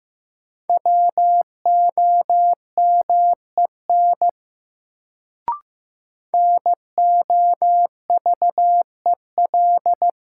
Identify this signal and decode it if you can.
Morse code, heard as a keyed tone, about 16 wpm, 700 Hz. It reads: WOMEN NOVEL
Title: Sentences from Top 1300 Words - New Word Drill Alt - 15wpm